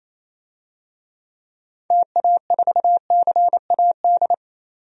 Morse code, heard as a keyed tone, about 27 wpm, 700 Hz. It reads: TA4CAD
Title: DXCC-100 Call Signs - 6 Characters - 28wpm